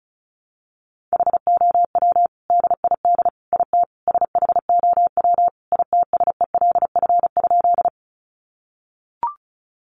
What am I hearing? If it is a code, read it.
Morse code, heard as a keyed tone, about 34 wpm, 700 Hz. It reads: HOW DID IT SHOW ITSELF?
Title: Sentences from Top 800 Words 35wpm